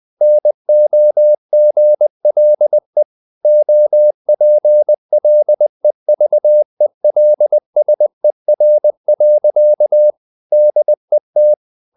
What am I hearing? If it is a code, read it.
Morse code, heard as a keyed tone, about 20 wpm, 600 Hz. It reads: NOGLE OPLEVELSER. DET
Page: Ferie 20 wpm